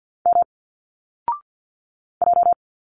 Morse code, heard as a keyed tone, about 51 wpm, 700 Hz. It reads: M J
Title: Single Letters 50wpm